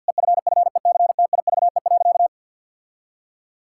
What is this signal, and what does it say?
Morse code, heard as a keyed tone, about 52 wpm, 700 Hz. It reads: EFFECTIVE.
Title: Sentences from Top 1000 Words - Review Part 6 - 50wpm